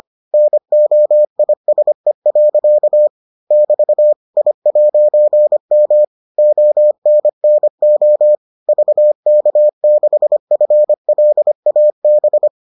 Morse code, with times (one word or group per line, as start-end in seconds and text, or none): 0.3-3.1 NOISE.
3.5-4.1 <BT>
4.4-6.0 I'M
6.4-8.4 ONNO
8.7-12.5 VK6FLAB